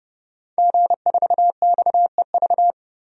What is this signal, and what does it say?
Morse code, heard as a keyed tone, about 30 wpm, 700 Hz. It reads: G4XEV
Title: DXCC-100 Call Signs - 5 Characters - 30wpm